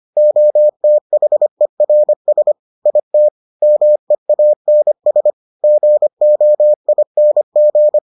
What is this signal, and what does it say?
Morse code, heard as a keyed tone, about 25 wpm, 600 Hz. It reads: OTHERS IT MEANS GOING